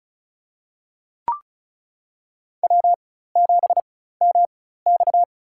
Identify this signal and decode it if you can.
Morse code, heard as a keyed tone, about 36 wpm, spaced as for 16 wpm, 700 Hz. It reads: W7MX
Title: USA Call Signs - 4 Characters - 4x Character Spacing - 35wpm